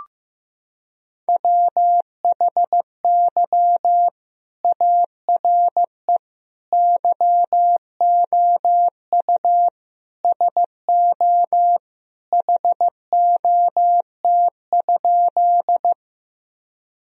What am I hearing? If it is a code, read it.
Morse code, heard as a keyed tone, about 15 wpm, 700 Hz. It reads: WHY ARE YOU SO HOT?
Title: Sentences from Top 1000 Words - Review Part 3 - 15wpm